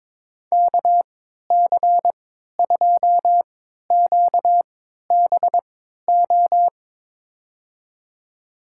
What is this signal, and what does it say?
Morse code, heard as a keyed tone, about 22 wpm, spaced as for 13 wpm, 700 Hz. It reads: KC2QBO
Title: USA Call Signs - 6 Characters - 3x Character Spacing - 22wpm